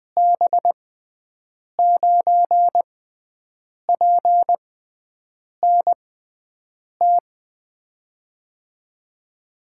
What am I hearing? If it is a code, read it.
Morse code, heard as a keyed tone, about 20 wpm, spaced as for 7 wpm, 700 Hz. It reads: B9PNT
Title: USA Call Signs - 6 Characters - 6x Character Spacing - 20wpm